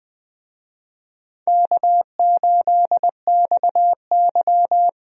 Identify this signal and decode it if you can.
Morse code, heard as a keyed tone, about 20 wpm, 700 Hz. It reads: K8XY
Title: USA Call Signs - 4 Characters - Repeated 3x - 20wpm